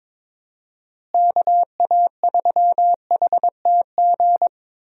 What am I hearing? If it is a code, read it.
Morse code, heard as a keyed tone, about 22 wpm, 700 Hz. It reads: KA3HTG
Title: USA Call Signs - 6 Characters - Repeated 3x - 22wpm